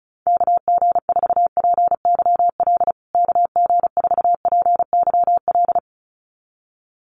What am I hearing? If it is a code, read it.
Morse code, heard as a keyed tone, about 34 wpm, 700 Hz. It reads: KG4PYL KG4PYL
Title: USA Call Signs Repeated 2x 35wpm